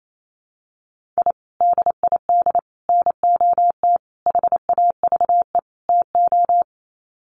Morse code, heard as a keyed tone, about 28 wpm, 700 Hz. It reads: I DID NOT HAVE TO
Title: Sets of 5 Words 28wpm